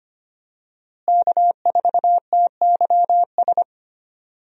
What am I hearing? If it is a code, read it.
Morse code, heard as a keyed tone, about 25 wpm, 700 Hz. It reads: K4TYS
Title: USA Call Signs - 5 Characters - Repeated 2x - 25wpm